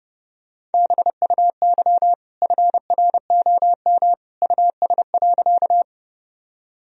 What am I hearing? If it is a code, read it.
Morse code, heard as a keyed tone, about 30 wpm, 700 Hz. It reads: BUY FROM US.
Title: Sentences from Top 1000 Words - Review Part 4 - 30wpm